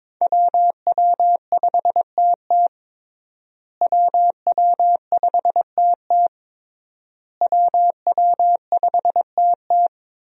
Morse code, heard as a keyed tone, about 22 wpm, 700 Hz. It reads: WW5TT WW5TT WW5TT
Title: USA Call Signs - 5 Characters - Repeated 3x - 22wpm